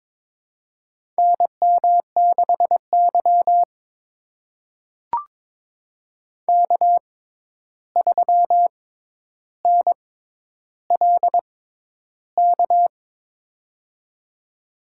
Morse code, heard as a keyed tone, about 22 wpm, spaced as for 11 wpm, 700 Hz. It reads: NM6Y K3NLK
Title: USA Call Signs - 6x Character Spacing - 22wpm